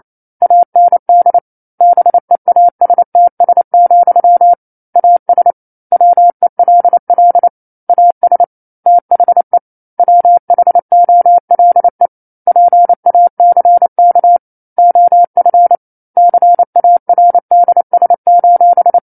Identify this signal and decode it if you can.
Morse code, heard as a keyed tone, about 28 wpm, 700 Hz. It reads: AND BEASTS, AS WELL AS THE WHOLE PACK OF CARDS:
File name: 09-Chapter-09 28 WPM